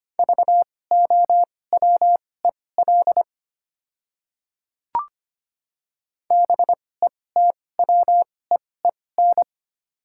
Morse code, heard as a keyed tone, about 25 wpm, spaced as for 18 wpm, 700 Hz. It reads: VOWEL BETWEEN
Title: 2x Character Spacing - Top 500 Words - 25wpm